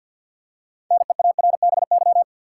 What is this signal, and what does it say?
Morse code, heard as a keyed tone, about 53 wpm, 700 Hz. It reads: NEARBY
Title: Six Letter Words - 50wpm